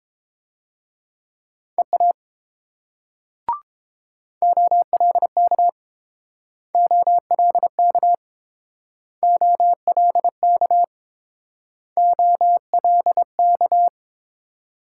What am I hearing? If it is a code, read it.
Morse code, heard as a keyed tone, about 29 wpm, 700 Hz. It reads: EA OLK OLK OLK OLK